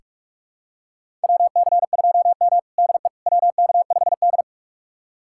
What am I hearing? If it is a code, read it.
Morse code, heard as a keyed tone, about 46 wpm, 700 Hz. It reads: WC2M DE WK5D
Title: USA Callsign DE Callsign - 4 Characters - 45wpm